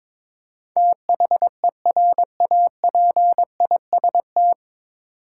Morse code, heard as a keyed tone, about 22 wpm, 700 Hz. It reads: THERAPIST
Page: Morse Code Ninja practice 22wpm